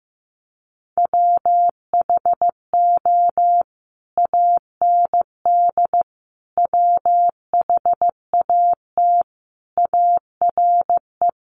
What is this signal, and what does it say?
Morse code, heard as a keyed tone, about 15 wpm, 700 Hz. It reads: WHO AND WHAT ARE
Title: Sentences from Top 100 Words 15wpm